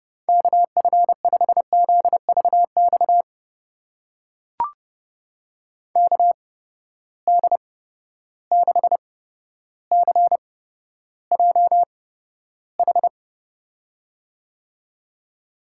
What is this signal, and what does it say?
Morse code, heard as a keyed tone, about 30 wpm, spaced as for 13 wpm, 700 Hz. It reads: KF5ZVX KD6CJH
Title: USA Call Signs - 6 Characters - 8x Character Spacing - 30wpm